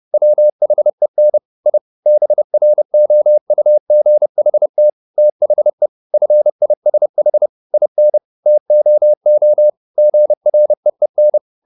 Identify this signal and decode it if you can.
Morse code, heard as a keyed tone, about 30 wpm, 600 Hz. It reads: WHEN I BROUGHT THE FISH IN TOO GREEN
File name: the old man and the sea / the old man and the sea 1 30wpm